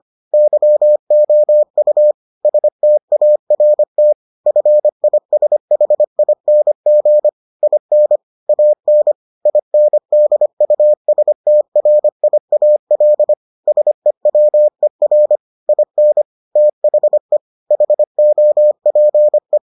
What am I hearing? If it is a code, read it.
Morse code, heard as a keyed tone, about 25 wpm, 600 Hz. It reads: YOU START FISHING IN AN INDUSTRIAL SEWER IN THE HOPE